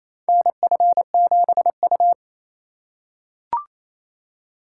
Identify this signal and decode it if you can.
Morse code, heard as a keyed tone, about 28 wpm, 700 Hz. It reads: NF7U
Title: USA Call Signs - 4 Characters - 3x Character Spacing - 28wpm